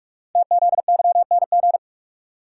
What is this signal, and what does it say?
Morse code, heard as a keyed tone, about 44 wpm, 700 Hz. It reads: TZYNG
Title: Code Groups - 5 Letter:Numbers - 45wpm